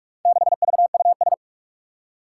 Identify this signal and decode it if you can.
Morse code, heard as a keyed tone, about 44 wpm, 700 Hz. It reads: 6VUS
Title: Four Challenging Letter-Number Combinations 45wpm